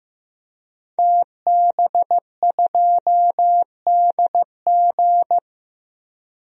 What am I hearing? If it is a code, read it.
Morse code, heard as a keyed tone, about 15 wpm, 700 Hz. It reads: TB2DG
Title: DXCC-100 Call Signs - 5 Characters - Repeated 2x - 15wpm